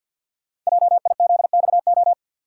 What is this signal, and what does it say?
Morse code, heard as a keyed tone, about 50 wpm, 700 Hz. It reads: JIZXO
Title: Code Groups - 5 Letters - 50wpm